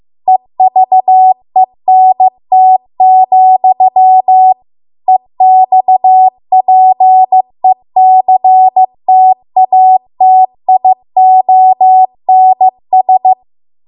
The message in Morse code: EVENT, EXPECTATIONS